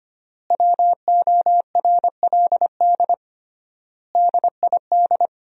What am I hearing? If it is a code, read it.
Morse code, heard as a keyed tone, about 25 wpm, 700 Hz. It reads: WORLD DID
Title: 3x Word-Spacing - Sentences from Top 200 Words - 25wpm